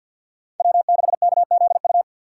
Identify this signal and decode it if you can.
Morse code, heard as a keyed tone, about 53 wpm, 700 Hz. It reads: W6KZU
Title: USA Call Signs - 4x Character Spacing - 50wpm